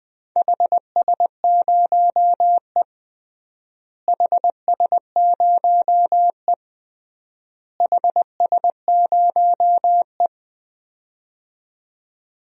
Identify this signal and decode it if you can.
Morse code, heard as a keyed tone, about 20 wpm, 690 Hz. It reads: HS0E HS0E HS0E
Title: DXCC-100 Call Signs - 4 Characters - Repeated 3x - 20wpm